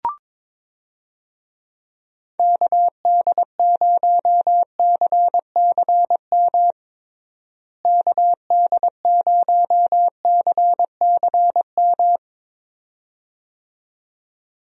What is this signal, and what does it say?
Morse code, heard as a keyed tone, about 22 wpm, 700 Hz. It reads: KD0CCM KD0CCM
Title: USA Call Signs - 6 Characters - Repeated 2x - 22wpm